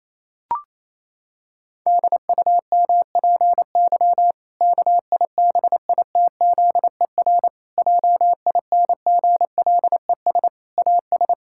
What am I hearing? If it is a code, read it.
Morse code, heard as a keyed tone, about 28 wpm, 700 Hz. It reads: DUMPY KIBITZER JINGLES AS
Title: Pangram Challenge v2 - 28wpm